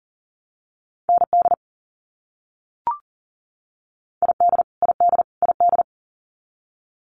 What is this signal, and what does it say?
Morse code, heard as a keyed tone, about 40 wpm, 700 Hz. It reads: ND ID ID ID
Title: US States Repeated 3x 40wpm